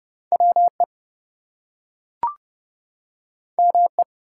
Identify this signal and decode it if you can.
Morse code, heard as a keyed tone, about 30 wpm, 700 Hz. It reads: WE ME
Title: Two Letter Words - 30wpm